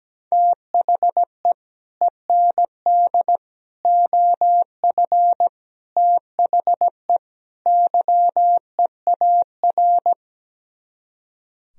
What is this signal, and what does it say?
Morse code, heard as a keyed tone, about 17 wpm, 700 Hz. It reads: THE END OF THE YEAR
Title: Speed-Racing - Sets of 5 Words 17wpm